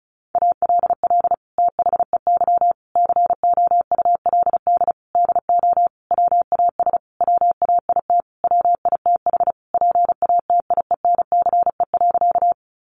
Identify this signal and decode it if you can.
Morse code, heard as a keyed tone, about 34 wpm, 700 Hz. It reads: ALL THEY COULD DO WAS WAIT WITH PATIENCE.
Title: Sentences from Top 1300 Words 35wpm